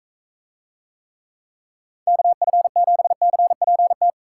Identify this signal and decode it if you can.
Morse code, heard as a keyed tone, about 44 wpm, 700 Hz. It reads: KF7CPT